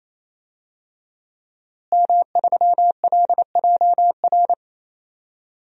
Morse code, heard as a keyed tone, about 28 wpm, 700 Hz. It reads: M3LJR
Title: DXCC-100 Call Signs - 5 Characters - Repeated 3x - 28wpm